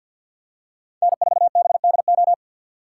Morse code, heard as a keyed tone, about 53 wpm, 690 Hz. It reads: N4BDO
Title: USA Call Signs - 5 Characters - 3x Character Spacing - 50wpm